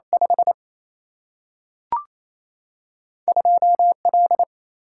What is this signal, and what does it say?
Morse code, heard as a keyed tone, about 28 wpm, 700 Hz. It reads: 5 2L
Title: Single-Letter Single-Number 28wpm